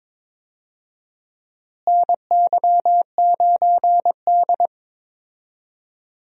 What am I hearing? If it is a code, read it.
Morse code, heard as a keyed tone, about 21 wpm, 700 Hz. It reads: NY9D
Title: USA Call Signs - 4 Characters - Repeated 2x - 22wpm